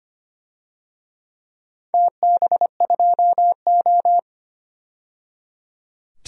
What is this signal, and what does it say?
Morse code, heard as a keyed tone, about 25 wpm, 700 Hz. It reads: TB2O
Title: DXCC-100 Call Signs - 4 Characters - 25wpm